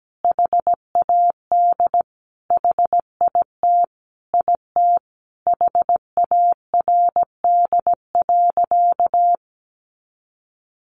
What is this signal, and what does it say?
Morse code, heard as a keyed tone, about 17 wpm, 700 Hz. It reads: HAD HIT IT HARD.
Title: Sentences from Top 1200 Words 17wpm